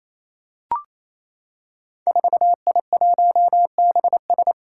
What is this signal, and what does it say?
Morse code, heard as a keyed tone, about 27 wpm, 700 Hz. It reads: 4I1BS
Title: Five Challenging Letter-Number Combinations 28wpm